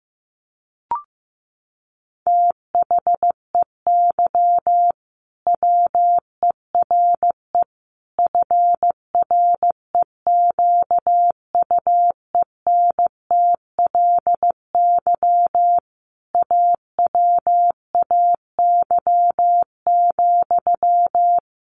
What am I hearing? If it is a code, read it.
Morse code, heard as a keyed tone, about 15 wpm, 700 Hz. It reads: THEY WERE FREQUENTLY AWAY,
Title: Sentences from Top 1400 Words 15wpm